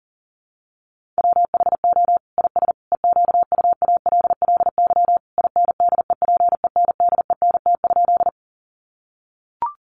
Morse code, heard as a keyed tone, about 37 wpm, 700 Hz. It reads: WHO IS EQUALLY INDEPENDENT?
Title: Sentences from Top 1400 Words 40wpm